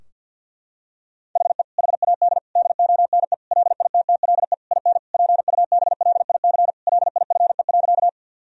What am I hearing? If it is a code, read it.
Morse code, heard as a keyed tone, about 50 wpm, 700 Hz. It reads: HE HAD DONE LITTLE IN PUBLIC LIFE.
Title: Sentences from Top 1000 Words - Review Part 3 - 50wpm